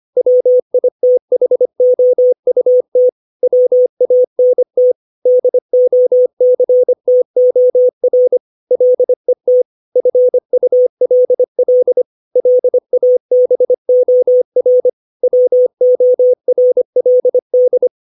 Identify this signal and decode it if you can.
Morse code, heard as a keyed tone, about 25 wpm, 500 Hz. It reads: WITHOUT WANT DOCTOR LET FULL LABOR WORLD
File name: 632 words / 632 Words 25WPM